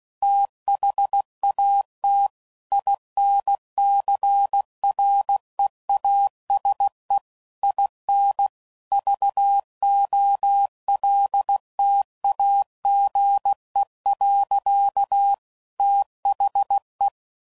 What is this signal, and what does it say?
Morse code, heard as a keyed tone, about 16 wpm, 800 Hz. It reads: THAT INCREASE IN VOLTAGE. THE
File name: CW Code Groups @15WPM / Technical Subjects / XC